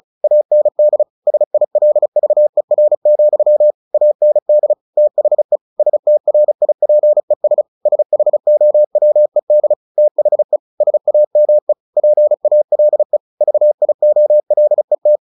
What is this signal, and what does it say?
Morse code, heard as a keyed tone, about 34 wpm, 600 Hz. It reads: AND SILVER, AND THE STRIPES SHOWED THE SAME PALE VIOLET
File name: the old man and the sea 2 35wpm